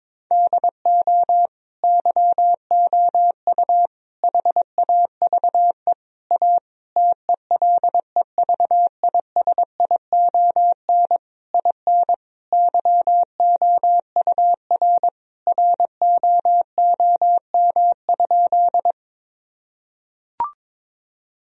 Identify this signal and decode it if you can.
Morse code, heard as a keyed tone, about 22 wpm, 700 Hz. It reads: DO YOU HAVE A TELEVISION IN YOUR ROOM?
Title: Sentences from Top 1000 Words - Review Part 4 - 22wpm